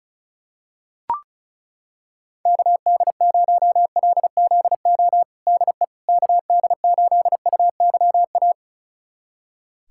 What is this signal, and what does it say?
Morse code, heard as a keyed tone, about 36 wpm, 700 Hz. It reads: KD0LZO DE KD8UYA
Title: USA Callsign DE Callsign - 6 Characters - 35wpm